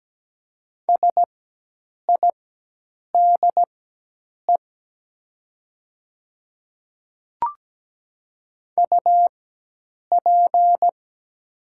Morse code, heard as a keyed tone, about 17 wpm, spaced as for 8 wpm, 700 Hz. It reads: SIDE UP
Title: 4x Character Spacing - Top 300 Words - 17wpm